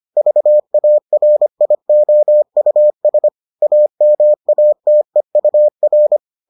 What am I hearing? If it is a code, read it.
Morse code, heard as a keyed tone, about 25 wpm, 600 Hz. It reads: VARIOUS AMATEUR